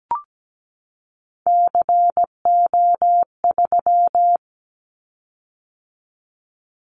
Morse code, heard as a keyed tone, about 17 wpm, 700 Hz. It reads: CO3
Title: Callsign Prefixes for Top 100 DXCC - 17wpm